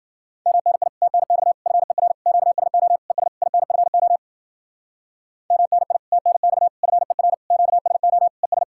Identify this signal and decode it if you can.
Morse code, heard as a keyed tone, about 52 wpm, 700 Hz. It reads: MNI TNX FER QSO ES INFO MNI TNX FER QSO ES